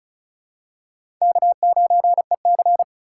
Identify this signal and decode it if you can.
Morse code, heard as a keyed tone, about 36 wpm, 700 Hz. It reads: K9EC
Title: USA Call Signs - 4 Characters - 4x Character Spacing - 35wpm